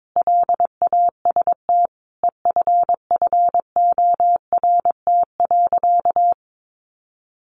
Morse code, heard as a keyed tone, about 22 wpm, 700 Hz. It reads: LAST EFFORT.
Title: Sentences from Top 1200 Words 22wpm